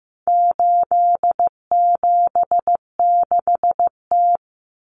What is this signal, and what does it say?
Morse code, heard as a keyed tone, about 15 wpm, 700 Hz. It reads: 876T
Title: Common Cut Numbers - 4-Digit 15wpm